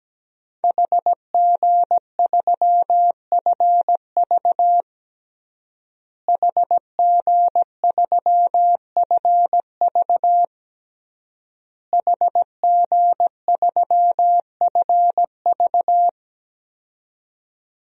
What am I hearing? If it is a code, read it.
Morse code, heard as a keyed tone, about 17 wpm, 700 Hz. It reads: HG3FV HG3FV HG3FV
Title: DXCC-100 Call Signs - 5 Characters - Repeated 3x - 17wpm